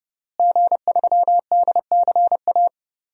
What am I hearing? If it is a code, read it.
Morse code, heard as a keyed tone, about 30 wpm, 700 Hz. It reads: G3DCA
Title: DXCC-100 Call Signs - 5 Characters - Repeated 3x - 30wpm